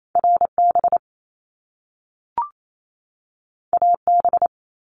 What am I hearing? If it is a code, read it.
Morse code, heard as a keyed tone, about 28 wpm, 700 Hz. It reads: RB AB
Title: Top Two Letter Combinations in Words 28wpm